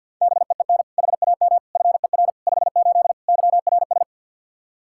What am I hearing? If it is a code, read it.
Morse code, heard as a keyed tone, about 53 wpm, 700 Hz. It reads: BEEN HAM FER 58 YRS
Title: Practice QSO - Exchange 3 - More Info - 50wpm